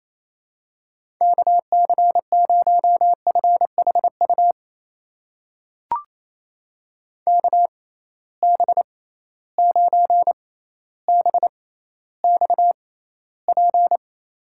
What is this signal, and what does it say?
Morse code, heard as a keyed tone, about 28 wpm, spaced as for 14 wpm, 700 Hz. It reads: KC0FHU KB9BXP
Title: USA Call Signs - 6 Characters - 6x Character Spacing - 28wpm